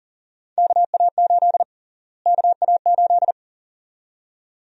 Morse code, heard as a keyed tone, about 40 wpm, 700 Hz. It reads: KA8 KA8
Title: USA Call Sign Prefixes - Repeated 2x - 40wpm